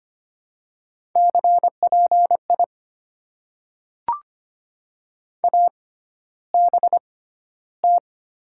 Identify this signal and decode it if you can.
Morse code, heard as a keyed tone, about 25 wpm, spaced as for 13 wpm, 700 Hz. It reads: CPI ABT
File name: Top 90 QSO Elements - 6x Character Spacing - 25wpm